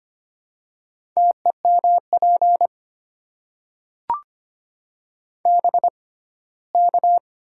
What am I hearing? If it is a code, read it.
Morse code, heard as a keyed tone, about 25 wpm, spaced as for 17 wpm, 700 Hz. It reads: TEMP BK
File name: Top 90 QSO Elements - 6x Character Spacing - 25wpm